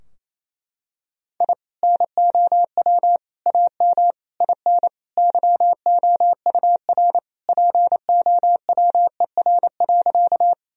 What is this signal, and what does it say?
Morse code, heard as a keyed tone, about 27 wpm, 700 Hz. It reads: I NOW AM IN YOUR POWER.
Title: Sentences from Top 1000 Words - Review Part 2 - 28wpm